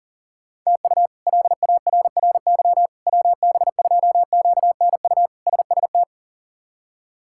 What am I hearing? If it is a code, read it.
Morse code, heard as a keyed tone, about 40 wpm, 700 Hz. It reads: TU LARRY WB2QNU SST
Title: K1USN SST QSO Practice - 40wpm